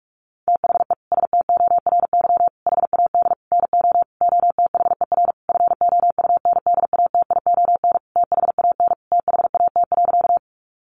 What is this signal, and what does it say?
Morse code, heard as a keyed tone, about 44 wpm, 700 Hz. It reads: THE STORY HAD NO OTHER FOUNDATION THAN THAT.
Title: Sentences from Top 1200 Words 45wpm